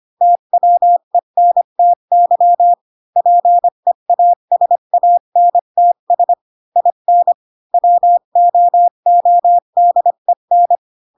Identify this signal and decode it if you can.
Morse code, heard as a keyed tone, about 25 wpm, 700 Hz. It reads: TWENTY PEASANTS IN WOODEN
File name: Pinocchio / Pinocchio ch3 20 25 ik1hge